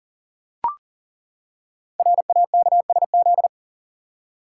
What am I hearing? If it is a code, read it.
Morse code, heard as a keyed tone, about 39 wpm, 700 Hz. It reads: RAKSZ
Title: Code Groups - 5 Letter:Numbers - 40wpm